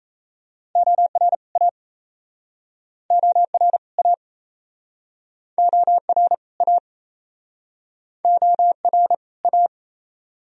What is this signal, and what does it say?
Morse code, heard as a keyed tone, about 39 wpm, 700 Hz. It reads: OR A OR A OR A OR A